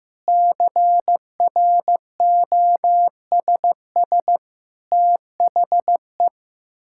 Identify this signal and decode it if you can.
Morse code, heard as a keyed tone, about 15 wpm, 700 Hz. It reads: CROSS THE
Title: Sentences from Top 1000 Words - Review Part 6 - 15wpm